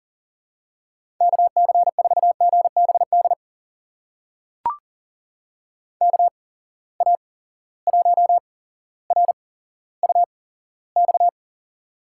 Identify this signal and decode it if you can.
Morse code, heard as a keyed tone, about 43 wpm, spaced as for 17 wpm, 700 Hz. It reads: KC4GBD KA1RUX
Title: USA Call Signs - 6 Characters - 8x Character Spacing - 40wpm